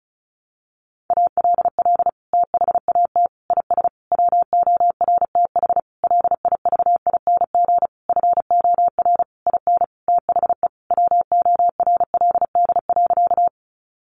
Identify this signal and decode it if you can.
Morse code, heard as a keyed tone, about 34 wpm, 700 Hz. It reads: ALL THAT IS WORTH LIVING FOR IN THE WORLD.
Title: Sentences from Top 1100 Words 35wpm